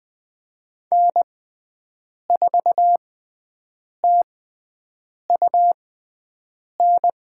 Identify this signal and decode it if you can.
Morse code, heard as a keyed tone, about 21 wpm, spaced as for 7 wpm, 700 Hz. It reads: N4TUN
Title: USA Call Signs - 5 Characters - 6x Character Spacing - 20wpm